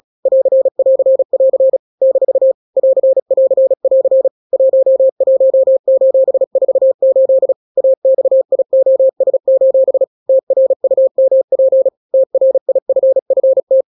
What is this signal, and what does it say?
Morse code, heard as a keyed tone, about 36 wpm, 530 Hz. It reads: <AR><AR><AR> - <AR><AR><AR> 11:48 AXIOS: TRUMP TRIFFT
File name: daily / 35wpm